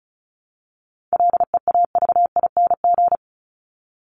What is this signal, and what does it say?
Morse code, heard as a keyed tone, about 34 wpm, 700 Hz. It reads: LEAVING
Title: ING Suffix - Top 100 Words with Base Words 35wpm